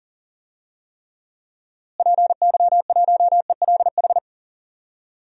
Morse code, heard as a keyed tone, about 40 wpm, 700 Hz. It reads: PY1ELH
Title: DXCC-100 Call Signs - 6 Characters - Repeated 3x - 40wpm